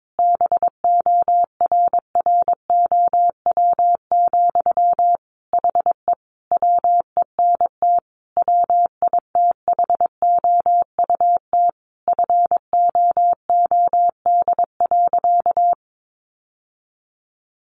Morse code, heard as a keyed tone, about 22 wpm, 700 Hz. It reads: BORROW, HE WENT WITHOUT FOOD.
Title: Sentences from Top 1500 Words 22wpm